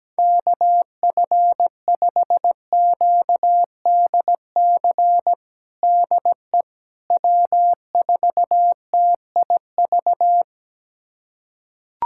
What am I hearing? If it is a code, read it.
Morse code, heard as a keyed tone, about 17 wpm, 700 Hz. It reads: KF5QDC DE W4TIV
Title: USA Callsign DE Callsign 17wpm